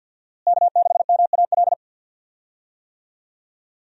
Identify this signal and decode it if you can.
Morse code, heard as a keyed tone, about 46 wpm, 700 Hz. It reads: K6MAL
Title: USA Call Signs - 5 Characters - Repeated 3x - 50wpm